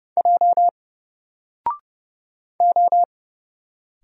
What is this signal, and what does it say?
Morse code, heard as a keyed tone, about 30 wpm, 700 Hz. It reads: J O
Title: Single Letter-Number Alt 30wpm